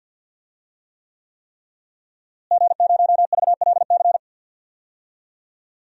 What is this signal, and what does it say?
Morse code, heard as a keyed tone, about 51 wpm, 700 Hz. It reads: G0VLC
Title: DXCC-100 Call Signs - 5 Characters - Repeated 3x - 50wpm